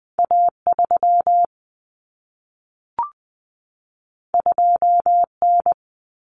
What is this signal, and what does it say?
Morse code, heard as a keyed tone, about 20 wpm, 700 Hz. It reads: A3 2N
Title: Common Cut Numbers - 2-Digit 20wpm